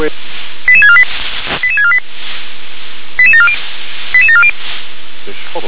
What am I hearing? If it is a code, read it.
Five-tone selective call (ZVEI), heard as quick runs of five tones, tones of about 70 ms, with speech. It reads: 88648 88648 88740 88740